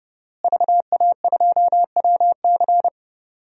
Morse code, heard as a keyed tone, about 30 wpm, 700 Hz. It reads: VA2WC
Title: DXCC-100 Call Signs - 5 Characters - Repeated 3x - 30wpm